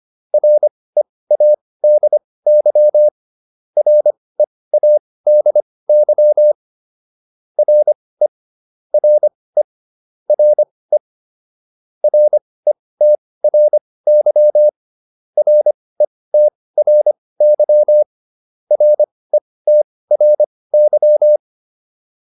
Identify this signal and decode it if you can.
Morse code, heard as a keyed tone, about 25 wpm, spaced as for 18 wpm, 600 Hz. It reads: READY READY RE RE RE RETRY RETRY RETRY